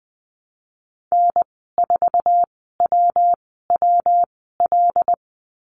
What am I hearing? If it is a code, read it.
Morse code, heard as a keyed tone, about 20 wpm, spaced as for 14 wpm, 700 Hz. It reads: N4WWL
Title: 122 Lesson 20g - Number 3 - Call Signs 20wpm